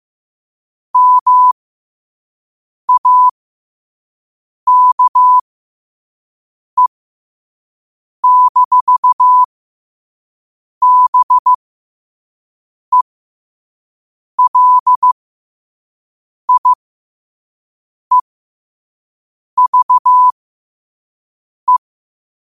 Morse code, heard as a keyed tone, about 15 wpm, spaced as for 5 wpm, 1000 Hz. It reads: MAKE-BELIEVE
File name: 5wpm